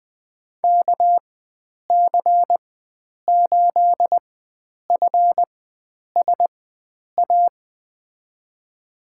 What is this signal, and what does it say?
Morse code, heard as a keyed tone, about 20 wpm, spaced as for 9 wpm, 700 Hz. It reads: KC8FSA
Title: USA Call Signs - 6 Characters - 4x Character Spacing - 20wpm